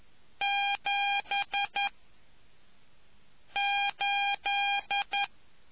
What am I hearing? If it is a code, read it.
Morse code, heard as a keyed tone, about 11 wpm, 800 Hz. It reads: 7 8